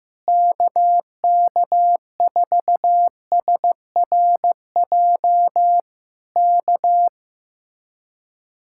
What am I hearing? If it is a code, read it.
Morse code, heard as a keyed tone, about 15 wpm, 700 Hz. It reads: KK4SRJ K
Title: Speed-Racing Common QSO Phrases Encore - 15wpm